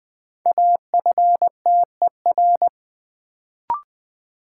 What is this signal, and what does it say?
Morse code, heard as a keyed tone, about 20 wpm, 700 Hz. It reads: AFTER
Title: Sentences from Top 200 Words - New Word Drill - 20wpm